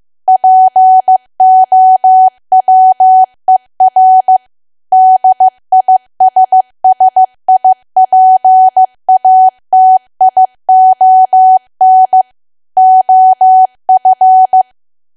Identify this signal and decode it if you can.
Morse code, heard as a keyed tone, about 15 wpm, 750 Hz. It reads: POWER DISSIPATION OF
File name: cwa / W1AW-Level2 / 15WPM